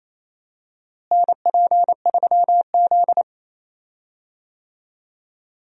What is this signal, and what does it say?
Morse code, heard as a keyed tone, about 28 wpm, 700 Hz. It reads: NP3Z
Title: USA Call Signs - 4 Characters - Repeated 3x - 28wpm